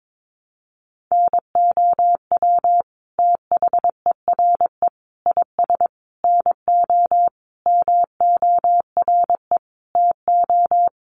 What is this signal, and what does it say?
Morse code, heard as a keyed tone, about 22 wpm, 700 Hz. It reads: NOW THERE IS NO MORE TO
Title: Sentences from Top 100 Words 22wpm